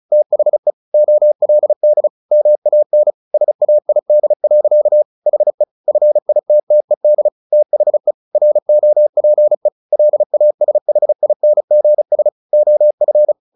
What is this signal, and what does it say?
Morse code, heard as a keyed tone, about 34 wpm, 600 Hz. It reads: THE OLD MAN SAID. HE FITTED THE ROPE LASHINGS OF
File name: the old man and the sea / the old man and the sea 1 35wpm